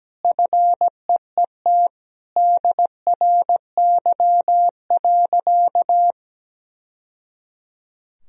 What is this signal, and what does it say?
Morse code, heard as a keyed tone, about 17 wpm, 700 Hz. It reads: FEET DRY.
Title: Sentences from Top 1000 Words - Review Part 3 - 17wpm